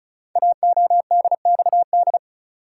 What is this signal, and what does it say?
Morse code, heard as a keyed tone, about 36 wpm, 690 Hz. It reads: AODXD
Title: Code Groups - 5 Letters - 35wpm